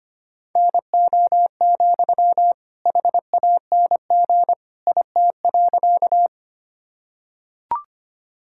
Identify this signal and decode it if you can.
Morse code, heard as a keyed tone, about 25 wpm, 700 Hz. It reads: NO, HANG IT.
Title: Sentences from Top 1600 Words 25wpm